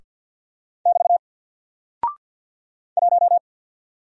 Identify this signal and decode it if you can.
Morse code, heard as a keyed tone, about 49 wpm, 700 Hz. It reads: <BT> 1
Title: Single Character (Letter:Number:Punctuation:Prosign) 50wpm